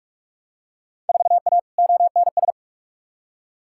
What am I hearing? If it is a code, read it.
Morse code, heard as a keyed tone, about 46 wpm, 700 Hz. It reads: 4A ONS
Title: Field Day Exchanges - 45wpm